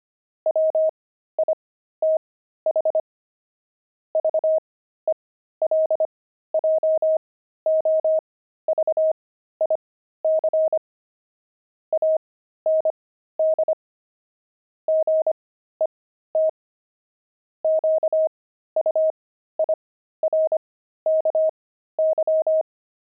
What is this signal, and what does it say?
Morse code, heard as a keyed tone, about 25 wpm, spaced as for 13 wpm, 630 Hz. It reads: WITH VELJOVIC AND GET QUIRKY
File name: pangram-fs-13-cs-25-zaph